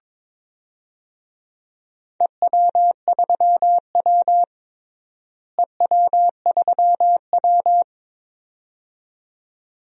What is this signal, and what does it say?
Morse code, heard as a keyed tone, about 22 wpm, 700 Hz. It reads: EW3W EW3W
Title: DXCC-100 Call Signs - 4 Characters - Repeated 2x - 22wpm